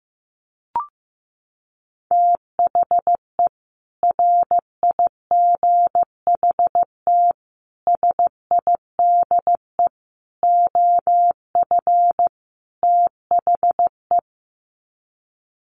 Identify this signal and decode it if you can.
Morse code, heard as a keyed tone, about 15 wpm, 700 Hz. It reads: THE RIGHT SIDE OF THE
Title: Sets of 5 Words Encore 15wpm